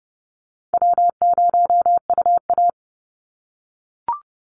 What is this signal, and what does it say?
Morse code, heard as a keyed tone, about 30 wpm, 700 Hz. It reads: W0UA
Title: Call Signs - 30wpm